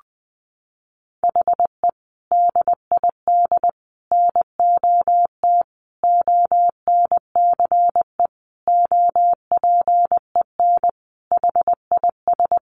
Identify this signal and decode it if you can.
Morse code, heard as a keyed tone, about 20 wpm, 700 Hz. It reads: HE DID NOT ONCE OPEN HIS
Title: Sentences from Top 1500 Words 20wpm